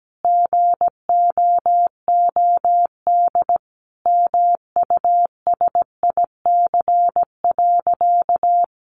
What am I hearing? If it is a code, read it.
Morse code, heard as a keyed tone, about 17 wpm, 700 Hz. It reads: GOOD MUSIC.
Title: Easy Phrases 17wpm